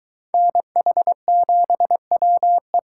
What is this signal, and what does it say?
Morse code, heard as a keyed tone, about 23 wpm, 700 Hz. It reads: NH7WE